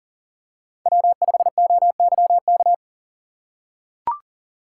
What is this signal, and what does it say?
Morse code, heard as a keyed tone, about 40 wpm, 700 Hz. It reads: W5OYK
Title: USA Call Signs - 5 Characters - 4x Character Spacing - 40wpm